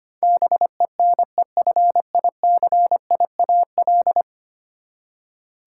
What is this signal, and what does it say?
Morse code, heard as a keyed tone, about 25 wpm, 700 Hz. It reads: BENEFICIAL
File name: Ten Letter Words - 25wpm